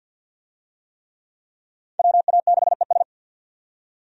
Morse code, heard as a keyed tone, about 50 wpm, 700 Hz. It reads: WA6ES
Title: USA Call Signs - 5 Characters - 50wpm